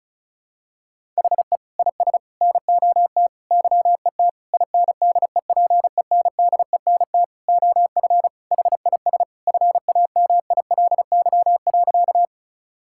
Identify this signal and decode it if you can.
Morse code, heard as a keyed tone, about 36 wpm, 700 Hz. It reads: HE IS NOT YET INDEPENDENT OF HIS FAMILY.
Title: Sentences from Top 1000 Words - Review Part 6 - 35wpm